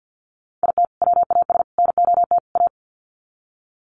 Morse code, heard as a keyed tone, about 50 wpm, 700 Hz. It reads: IT WAS NOT A
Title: Sets of 4 Words 50wpm